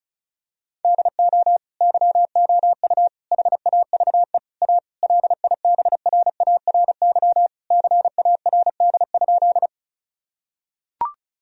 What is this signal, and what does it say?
Morse code, heard as a keyed tone, about 34 wpm, 700 Hz. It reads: DO YOU HAVE A LIBRARY CARD?
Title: Sentences from Top 1000 Words - Review Part 4 - 35wpm